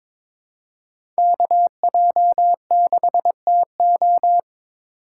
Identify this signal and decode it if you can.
Morse code, heard as a keyed tone, about 22 wpm, 700 Hz. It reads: KJ6TO
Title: USA Call Signs - 2x Character Spacing - 22wpm